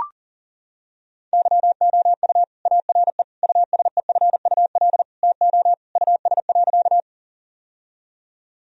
Morse code, heard as a keyed tone, about 40 wpm, 700 Hz. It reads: YOU ARE USEFUL TO US.
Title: Sentences from Top 1000 Words - Review Part 4 - 40wpm